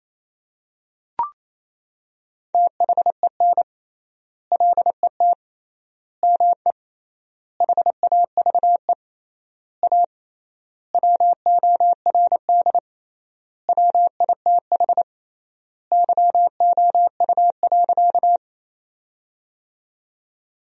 Morse code, heard as a keyed tone, about 28 wpm, 700 Hz. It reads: THEN LET ME HAVE A WORD WITH YOU.
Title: 3x Word-Spacing - Sentences from Top 300 Words - 28wpm